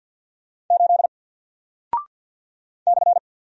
Morse code, heard as a keyed tone, about 48 wpm, 700 Hz. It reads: 8 /
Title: Single Character (Letter:Number:Punctuation:Prosign) Alt 50wpm